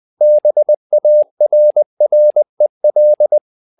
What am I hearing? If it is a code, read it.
Morse code, heard as a keyed tone, about 20 wpm, 600 Hz. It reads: BARREL
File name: tower trouble 20wpm